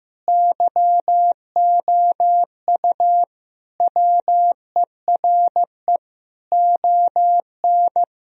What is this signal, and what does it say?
Morse code, heard as a keyed tone, about 15 wpm, 700 Hz. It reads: YOU WERE ON
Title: Sentences from Top 1000 Words - Review Part 2 - 15wpm